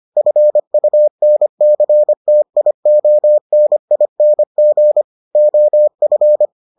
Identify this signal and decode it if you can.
Morse code, heard as a keyed tone, about 25 wpm, 600 Hz. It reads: FUNCTIONING OF